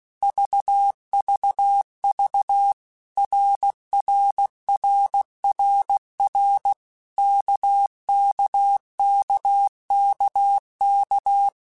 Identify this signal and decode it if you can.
Morse code, heard as a keyed tone, about 16 wpm, 800 Hz. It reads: VVV RRRRR KKKKK
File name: Basic Letter Groups By7Groups / RKPX / RKPX@15